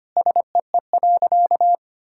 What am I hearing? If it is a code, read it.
Morse code, heard as a keyed tone, about 25 wpm, 700 Hz. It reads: SEE.
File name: practice / 25wpm / Sentences with 4-Word Phrases - 25wpm